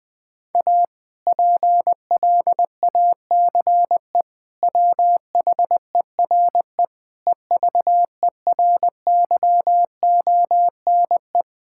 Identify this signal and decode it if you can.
Morse code, heard as a keyed tone, about 20 wpm, 700 Hz. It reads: A PLACE WHERE EVERYONE
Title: Sentences from Top 1000 Words - Review Part 5 - 20wpm